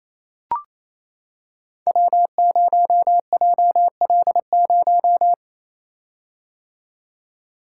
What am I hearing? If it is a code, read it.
Morse code, heard as a keyed tone, about 29 wpm, 700 Hz. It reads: W0JL0
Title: Code Groups - 5 Letter:Numbers - 28wpm